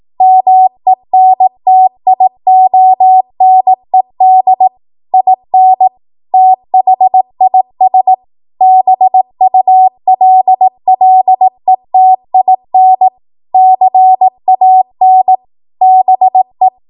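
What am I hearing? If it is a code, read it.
Morse code, heard as a keyed tone, about 18 wpm, 750 Hz. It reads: MENTIONED IN THIS BULLETIN CAN BE
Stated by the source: ARRL 18 wpm Morse code practice archive transmission.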